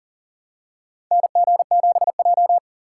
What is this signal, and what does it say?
Morse code, heard as a keyed tone, about 40 wpm, 700 Hz. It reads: NG7J
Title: USA Call Signs - 2x Character Spacing - 40wpm